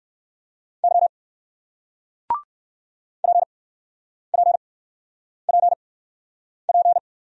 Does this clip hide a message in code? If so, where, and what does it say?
Yes: Morse code from 0.8 to 7.0 s, heard as a keyed tone, about 60 wpm, spaced as for 9 wpm, 700 Hz. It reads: Y PPPP